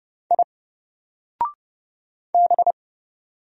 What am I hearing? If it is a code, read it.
Morse code, heard as a keyed tone, about 30 wpm, 700 Hz. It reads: I B
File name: Single-Letter-Number-30wpm